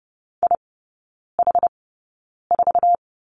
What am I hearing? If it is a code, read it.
Morse code, heard as a keyed tone, about 30 wpm, spaced as for 9 wpm, 700 Hz. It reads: IH4
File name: 081 Lesson 14e - Number 2 - Three Characters 30wpm